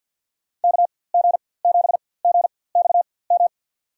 Morse code, heard as a keyed tone, about 46 wpm, spaced as for 23 wpm, 700 Hz. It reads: KG7GXM
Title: USA Call Signs - 4x Character Spacing - 50wpm